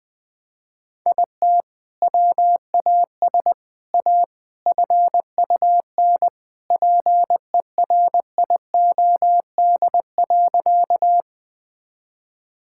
Morse code, heard as a keyed tone, about 21 wpm, 700 Hz. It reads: IT WAS A FUN PERIOD.
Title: Sentences from Top 1000 Words - Review Part 5 - 20wpm